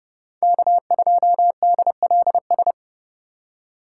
Morse code, heard as a keyed tone, about 29 wpm, 700 Hz. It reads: K2DLS